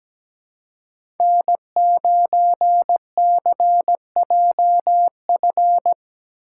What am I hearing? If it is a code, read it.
Morse code, heard as a keyed tone, about 17 wpm, 700 Hz. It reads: N9CJF
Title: USA Call Signs - 5 Characters - 2x Character Spacing - 17wpm